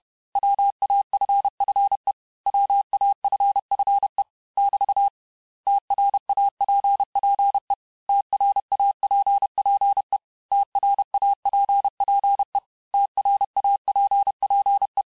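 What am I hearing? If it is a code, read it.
Morse code, heard as a keyed tone, about 31 wpm, 800 Hz. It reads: WAFFE WAFFE <BT> TRAPPE TRAPPE TRAPPE TRAPPE
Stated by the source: Morse lesson of words, each word repeated 4 times, at 29 words per minute.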